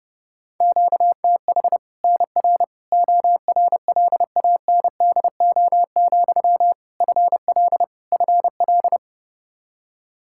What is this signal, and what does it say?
Morse code, heard as a keyed tone, about 30 wpm, 700 Hz. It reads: QTH NR ORLANDO, FL FL
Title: Practice QSO - Exchange 1 - Beginning - 30wpm